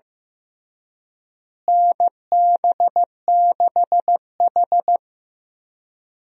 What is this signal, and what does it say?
Morse code, heard as a keyed tone, about 15 wpm, 700 Hz. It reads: NB6H
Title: USA Call Signs - 4 Characters - Repeated 3x - 15wpm